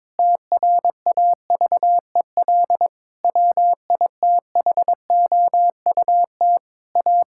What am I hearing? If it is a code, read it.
Morse code, heard as a keyed tone, about 22 wpm, 700 Hz. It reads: TRAVEL WITHOUT A
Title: Sentences from Top 1000 Words - Review Part 6 - 22wpm